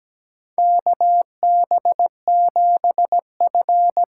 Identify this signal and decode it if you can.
Morse code, heard as a keyed tone, about 17 wpm, 700 Hz. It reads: KB7F
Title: USA Call Signs - 4 Characters - 4x Character Spacing - 17wpm